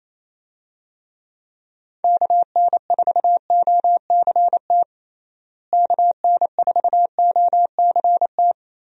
Morse code, heard as a keyed tone, about 28 wpm, 700 Hz. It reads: KN4OCT KN4OCT
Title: USA Call Signs - 6 Characters - Repeated 3x - 28wpm